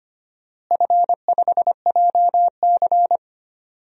Morse code, heard as a keyed tone, about 25 wpm, 700 Hz. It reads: F5JC
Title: DXCC-100 Call Signs - 4 Characters - Repeated 3x - 25wpm